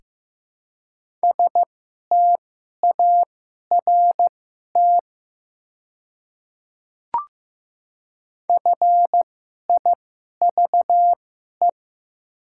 Morse code, heard as a keyed tone, about 15 wpm, spaced as for 11 wpm, 700 Hz. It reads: START FIVE
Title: 2x Character Spacing - Top 500 Words - 15wpm